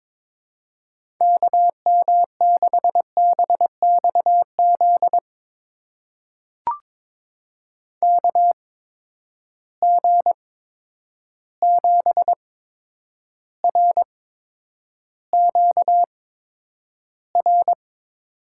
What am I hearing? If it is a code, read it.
Morse code, heard as a keyed tone, about 22 wpm, spaced as for 9 wpm, 700 Hz. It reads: KM6BXZ KG7RQR
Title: USA Call Signs - 6 Characters - 8x Character Spacing - 22wpm